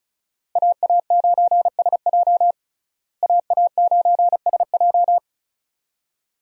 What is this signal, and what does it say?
Morse code, heard as a keyed tone, about 36 wpm, 700 Hz. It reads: AA9SJ AA9SJ
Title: USA Call Signs - 5 Characters - Repeated 2x - 35wpm